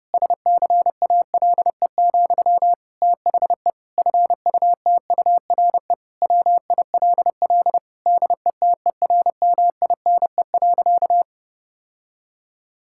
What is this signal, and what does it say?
Morse code, heard as a keyed tone, about 30 wpm, 700 Hz. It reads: SCALE, THE FUTURE WILL DETERMINE.
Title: Sentences from Top 1000 Words - Review Part 5 - 30wpm